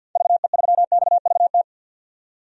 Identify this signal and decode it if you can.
Morse code, heard as a keyed tone, about 52 wpm, 700 Hz. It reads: VE3XVT
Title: DXCC-100 Call Signs - 6 Characters - 50wpm